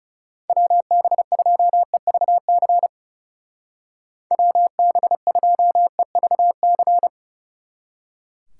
Morse code, heard as a keyed tone, about 36 wpm, 700 Hz. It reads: WB2EVC WB2EVC
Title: Speed Racing - USA 2x3 Call Signs 30wpm